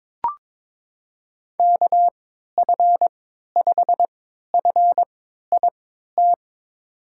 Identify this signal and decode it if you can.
Morse code, heard as a keyed tone, about 22 wpm, spaced as for 12 wpm, 700 Hz. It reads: KF5FIT
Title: USA Call Signs - 6 Characters - 3x Character Spacing - 22wpm